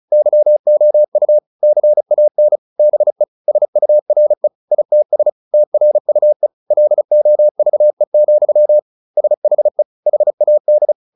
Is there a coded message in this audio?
Yes: Morse code from 0.1 to 10.9 s, heard as a keyed tone, about 34 wpm, 600 Hz. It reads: YOU CAN BE SURE ITS TRUE LOVE, SHE HAD